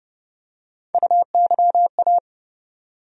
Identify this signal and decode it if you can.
Morse code, heard as a keyed tone, about 30 wpm, 700 Hz. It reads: UYA
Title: USA Call Sign Postfixes - 3 Letters 30wpm